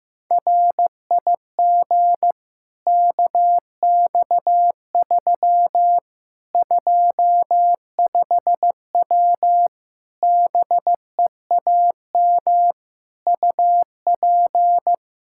Morse code, heard as a keyed tone, about 16 wpm, 700 Hz. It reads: RIG KX3 25W BEAM UP
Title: K1USN SST QSO Practice 15wpm